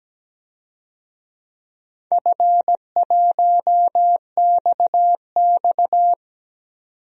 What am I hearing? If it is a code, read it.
Morse code, heard as a keyed tone, about 17 wpm, 700 Hz. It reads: F1XX
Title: DXCC-100 Call Signs - 4 Characters - 17wpm